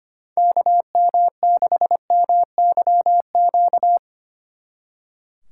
Morse code, heard as a keyed tone, about 31 wpm, 700 Hz. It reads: KM6MYQ